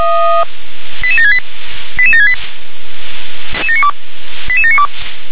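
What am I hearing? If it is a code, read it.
Five-tone selective call (ZVEI), heard as quick runs of five tones, tones of about 70 ms. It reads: 88768 88768 88792
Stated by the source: ZVEI five-tone selective call received with an SDR.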